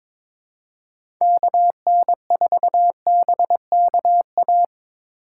Morse code, heard as a keyed tone, about 25 wpm, 700 Hz. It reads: KN4BKA
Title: Speed Racing - USA 2x3 Call Signs 22wpm